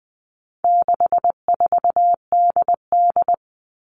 Morse code, heard as a keyed tone, about 20 wpm, 700 Hz. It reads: 64DD
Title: All Cut Numbers - 4-Digit 20wpm